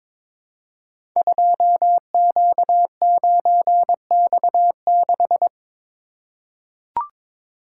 Morse code, heard as a keyed tone, about 22 wpm, 700 Hz. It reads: 2Q9X6
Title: Code Groups - 5 Letter:Numbers - 22wpm